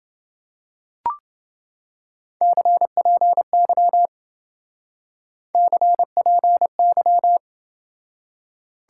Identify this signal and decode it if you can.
Morse code, heard as a keyed tone, about 27 wpm, 700 Hz. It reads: CPY CPY